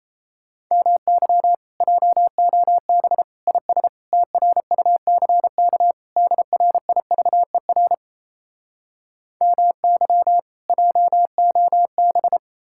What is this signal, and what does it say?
Morse code, heard as a keyed tone, about 37 wpm, 700 Hz. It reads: MY JOB IS TRUCK DRIVER MY JOB